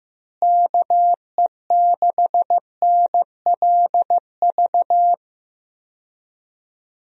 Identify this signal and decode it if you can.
Morse code, heard as a keyed tone, about 15 wpm, 700 Hz. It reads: KE6NLV
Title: USA Call Signs - 6 Characters - Repeated 3x - 15wpm